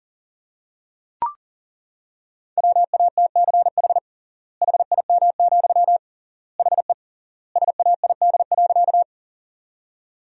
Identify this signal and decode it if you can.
Morse code, heard as a keyed tone, about 40 wpm, 700 Hz. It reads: WATCH HIM, HE SAID.
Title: 3x Word-Spacing - Sentences from Top 300 Words - 40wpm